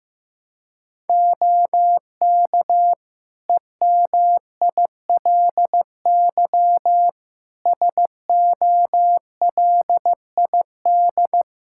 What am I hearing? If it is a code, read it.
Morse code, heard as a keyed tone, about 15 wpm, 700 Hz. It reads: OK EMILY SOLID
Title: Practice QSO - Exchange 3 - More Info - 15wpm